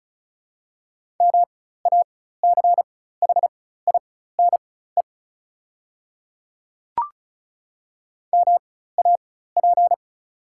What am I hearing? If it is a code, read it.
Morse code, heard as a keyed tone, about 36 wpm, spaced as for 16 wpm, 700 Hz. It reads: MACHINE MAP
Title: 4x Character Spacing - Top 500 Words - 35wpm